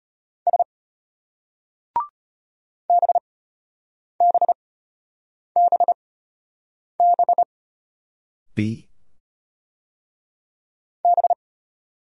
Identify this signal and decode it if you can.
Morse code, heard as a keyed tone, about 33 wpm, spaced as for 8 wpm, 700 Hz. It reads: S BBBB B